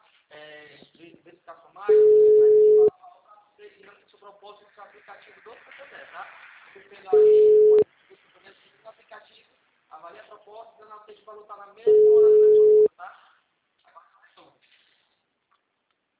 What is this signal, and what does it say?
Ringback tone (Europe, 425 Hz), with speech.
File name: out- -2002- - - .14652